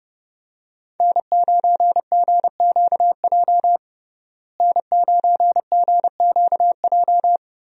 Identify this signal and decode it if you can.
Morse code, heard as a keyed tone, about 32 wpm, 700 Hz. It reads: N9GQJ N9GQJ
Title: USA Call Signs - 5 Characters - Repeated 3x - 30wpm